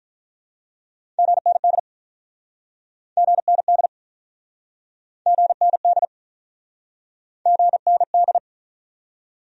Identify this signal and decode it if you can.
Morse code, heard as a keyed tone, about 46 wpm, 700 Hz. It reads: GND GND GND GND